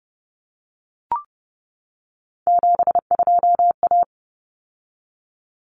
Morse code, heard as a keyed tone, about 30 wpm, 700 Hz. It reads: 72A
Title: Common Cut Numbers - 3-Digit 30wpm